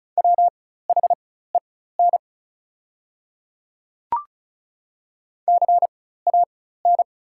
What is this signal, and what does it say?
Morse code, heard as a keyed tone, about 36 wpm, spaced as for 16 wpm, 700 Hz. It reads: WHEN CAN
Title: 4x Character Spacing - Top 100 Words - 35wpm